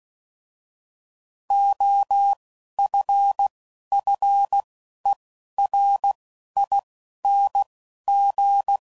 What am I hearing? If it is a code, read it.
Morse code, heard as a keyed tone, about 16 wpm, 800 Hz. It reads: O F F E R I N G
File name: Suffixes / Special INGs / Single INGs